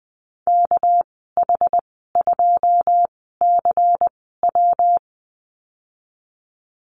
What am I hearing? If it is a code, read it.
Morse code, heard as a keyed tone, about 20 wpm, spaced as for 14 wpm, 700 Hz. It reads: KH2CW
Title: 200 Lesson 32g - Letter k - Call Signs 20wpm